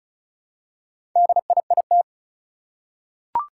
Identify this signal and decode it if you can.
Morse code, heard as a keyed tone, about 36 wpm, 700 Hz. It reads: DIIT
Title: Four Challenging Letter-Number Combinations 35wpm